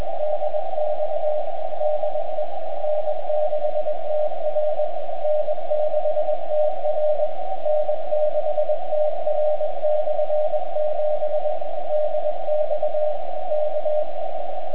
Morse code, heard as a keyed tone, about 21 wpm, 600 Hz, with radio noise. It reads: DX DE N6TR N6TR N6TR/7 DX K
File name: n6tr / N6TR 7 160m at RW4PL 01-09-09 1329z